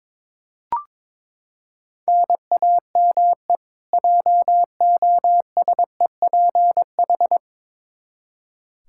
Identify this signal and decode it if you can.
Morse code, heard as a keyed tone, about 22 wpm, 700 Hz. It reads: NAME JOSEPH
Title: K1USN SST QSO Practice 22wpm